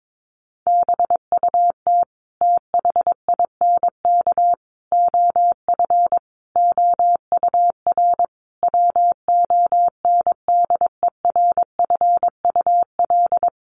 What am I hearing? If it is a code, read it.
Morse code, heard as a keyed tone, about 22 wpm, 700 Hz. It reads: BUT THINK OF OUR WONDERFUL
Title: Sentences from Top 1100 Words 22wpm